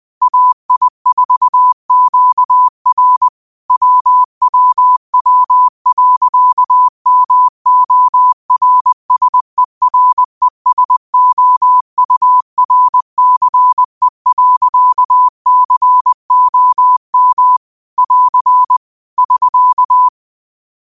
Morse code, heard as a keyed tone, about 21 wpm, 1000 Hz. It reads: AI4QR WWW.MORSERESOURCE.COM <AR> <SK>
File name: 20wpm